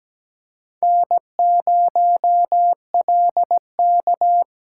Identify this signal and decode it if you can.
Morse code, heard as a keyed tone, about 17 wpm, 700 Hz. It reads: N0LK
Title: USA Call Signs - 4x Character Spacing - 17wpm